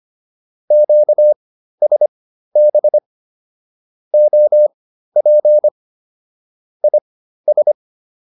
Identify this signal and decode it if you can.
Morse code, heard as a keyed tone, about 25 wpm, spaced as for 13 wpm, 600 Hz. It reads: QSB OP IS